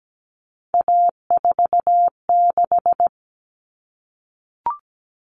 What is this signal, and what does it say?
Morse code, heard as a keyed tone, about 17 wpm, 700 Hz. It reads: A46
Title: Common Cut Numbers - 3-Digit 17wpm